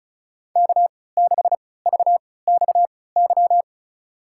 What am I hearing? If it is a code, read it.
Morse code, heard as a keyed tone, about 34 wpm, spaced as for 20 wpm, 700 Hz. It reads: K6VXY
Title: USA Call Signs - 5 Characters - 3x Character Spacing - 35wpm